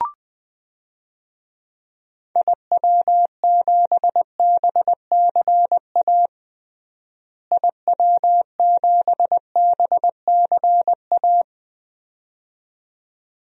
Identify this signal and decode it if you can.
Morse code, heard as a keyed tone, about 21 wpm, 700 Hz. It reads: IW7BCA IW7BCA
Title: DXCC-100 Call Signs - 6 Characters - Repeated 2x - 20wpm